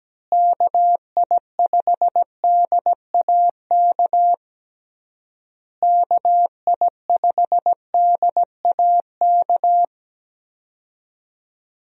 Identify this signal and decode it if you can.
Morse code, heard as a keyed tone, about 17 wpm, 700 Hz. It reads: KI5DAK KI5DAK
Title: USA Call Signs - 6 Characters - Repeated 3x - 17wpm